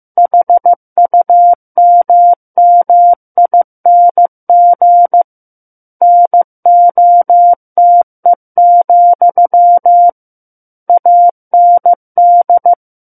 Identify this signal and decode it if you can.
Morse code, heard as a keyed tone, about 15 wpm, 700 Hz. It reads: HUMMING NOTE, AND
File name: Chapter 6 - 15 WPM